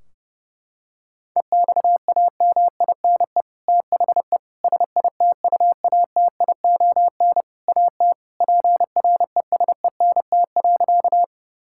Morse code, heard as a keyed tone, about 30 wpm, 700 Hz. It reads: EXAMINE THE SITUATION AT PRESENT.
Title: Sentences from Top 1600 Words 30wpm